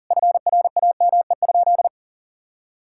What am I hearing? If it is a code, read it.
Morse code, heard as a keyed tone, about 40 wpm, 700 Hz. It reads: FRAME?
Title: Sentences from Top 1000 Words - Review Part 6 - 40wpm